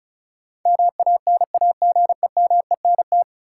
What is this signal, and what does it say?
Morse code, heard as a keyed tone, about 36 wpm, 700 Hz. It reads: MANAGEMENT
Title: Sentences from Top 1200 Words - New Word Drill Alt - 35wpm